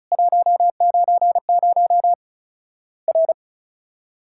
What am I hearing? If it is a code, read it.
Morse code, heard as a keyed tone, about 36 wpm, 700 Hz. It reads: 190 R
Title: Worked All Europe DX Contest - QTC Exchange - 35wpm